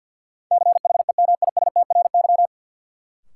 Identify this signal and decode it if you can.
Morse code, heard as a keyed tone, about 48 wpm, 700 Hz. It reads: CHEMISTRY
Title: Sentences from Top 1000 Words - Review Part 6 - New Word Drill - 50wpm